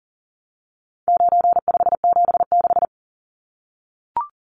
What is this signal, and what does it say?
Morse code, heard as a keyed tone, about 40 wpm, 700 Hz. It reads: 9576
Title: Numbers - 4-Digit 40wpm